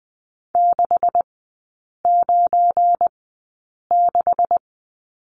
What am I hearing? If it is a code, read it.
Morse code, heard as a keyed tone, about 20 wpm, spaced as for 8 wpm, 700 Hz. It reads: 696
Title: 172 Lesson 28e - Number 9 - Three Characters 20wpm